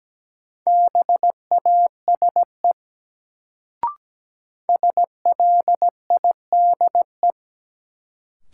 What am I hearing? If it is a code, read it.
Morse code, heard as a keyed tone, about 17 wpm, 700 Hz. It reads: BASE SLIDE
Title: Sentences from Top 1600 Words - New Word Drill Alt - 17wpm